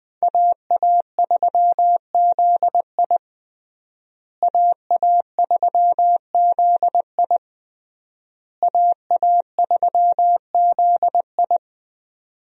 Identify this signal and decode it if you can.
Morse code, heard as a keyed tone, about 20 wpm, 700 Hz. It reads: AA3ZI AA3ZI AA3ZI
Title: USA Call Signs - 5 Characters - Repeated 3x - 20wpm